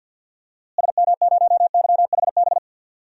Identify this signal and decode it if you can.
Morse code, heard as a keyed tone, about 47 wpm, 700 Hz. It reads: SM0YHB
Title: DXCC-100 Call Signs - 6 Characters - Repeated 2x - 50wpm